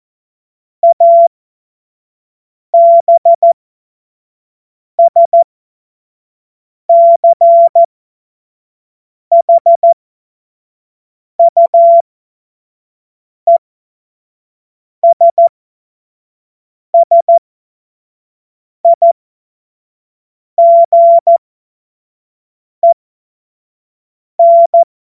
Morse code, heard as a keyed tone, about 14 wpm, spaced as for 5 wpm, 680 Hz. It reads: ABSCHUESSIGEN
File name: Morsekurs